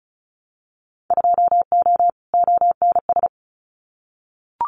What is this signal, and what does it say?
Morse code, heard as a keyed tone, about 36 wpm, 700 Hz. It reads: 2O ONS
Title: Winter Field Day Exchanges Encore 35wpm